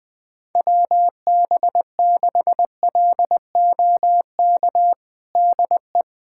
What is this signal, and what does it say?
Morse code, heard as a keyed tone, about 20 wpm, 700 Hz. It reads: WB6LOK DE
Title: USA Callsign DE Callsign - 6 Characters - 20wpm